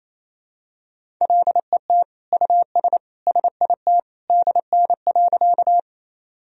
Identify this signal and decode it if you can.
Morse code, heard as a keyed tone, about 28 wpm, 700 Hz. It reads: LET US SIT DN.
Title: Sentences from Top 500 Words and Abbreviations 28wpm